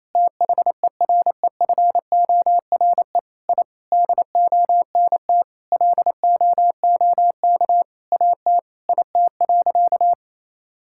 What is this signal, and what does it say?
Morse code, heard as a keyed tone, about 28 wpm, 700 Hz. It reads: THEREFORE I DONT LOOK AT IT.
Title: Sentences from Top 1000 Words - Review Part 5 - 28wpm